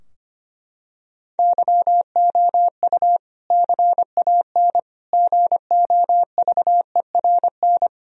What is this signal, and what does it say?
Morse code, heard as a keyed tone, about 25 wpm, 700 Hz. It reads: YOU CAN GOVERN
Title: Sentences from Top 1000 Words - Review Part 2 - 25wpm